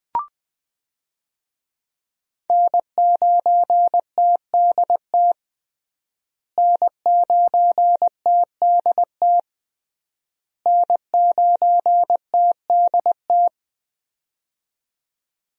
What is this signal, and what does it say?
Morse code, heard as a keyed tone, about 20 wpm, 700 Hz. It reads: N9TDT N9TDT N9TDT
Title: USA Call Signs - 5 Characters - Repeated 3x - 20wpm